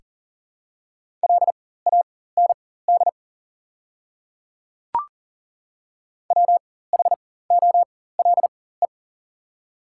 Morse code, heard as a keyed tone, about 39 wpm, spaced as for 19 wpm, 700 Hz. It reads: LAND WHOLE
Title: 4x Character Spacing - Top 500 Words - 40wpm